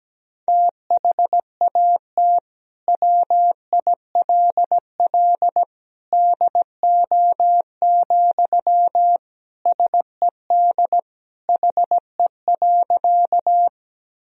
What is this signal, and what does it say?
Morse code, heard as a keyed tone, about 17 wpm, 700 Hz. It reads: THAT WILL DO, SED HE.
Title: Sentences from Top 100 Words and Abbreviations 17wpm